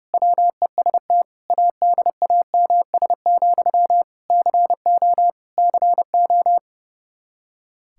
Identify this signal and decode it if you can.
Morse code, heard as a keyed tone, about 30 wpm, 700 Hz. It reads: WEST ADAMS, CO CO
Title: K1USN SST QSO Practice 30wpm